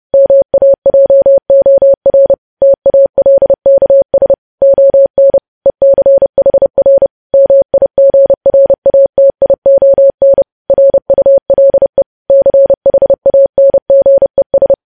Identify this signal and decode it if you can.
Morse code, heard as a keyed tone, about 30 wpm, 550 Hz. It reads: MAJOR TALKS ON ECHR MIGRATION RULE CHANGES